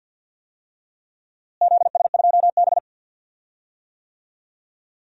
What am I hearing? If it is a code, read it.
Morse code, heard as a keyed tone, about 52 wpm, 700 Hz. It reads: ZS2B
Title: DXCC-100 Call Signs - 4 Characters - Repeated 2x - 50wpm